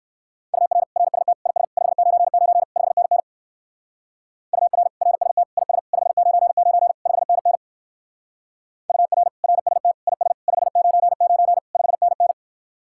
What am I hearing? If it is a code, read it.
Morse code, heard as a keyed tone, about 58 wpm, 700 Hz. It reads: UR RST IS 599 5NN UR RST IS 599 5NN UR RST IS 599 5NN